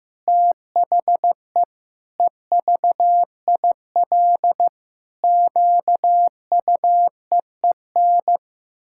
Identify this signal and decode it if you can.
Morse code, heard as a keyed tone, about 15 wpm, 700 Hz. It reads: THE EVIL QUEEN
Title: Pangram Challenge v2 - 15wpm